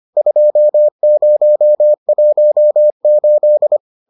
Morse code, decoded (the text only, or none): 2018